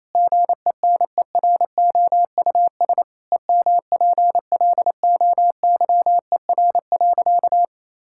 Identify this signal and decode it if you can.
Morse code, heard as a keyed tone, about 27 wpm, 700 Hz. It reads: GENEROUS EMPLOYER.
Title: Pangram Challenge v2 - 28wpm